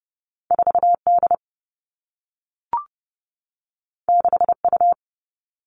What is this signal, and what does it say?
Morse code, heard as a keyed tone, about 30 wpm, 700 Hz. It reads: 4 D 6 U
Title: Two Challenging Letter-Number Combinations 30wpm